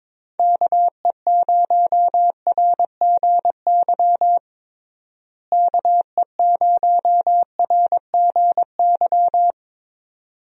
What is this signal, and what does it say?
Morse code, heard as a keyed tone, about 22 wpm, 700 Hz. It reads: KE0RGY KE0RGY
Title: USA Call Signs - 6 Characters - Repeated 2x - 22wpm